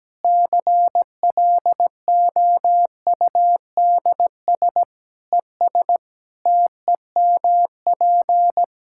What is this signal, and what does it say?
Morse code, heard as a keyed tone, about 17 wpm, 700 Hz. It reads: CLOUDS ES TEMP
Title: Practice QSO - Exchange 2 - Info - 17wpm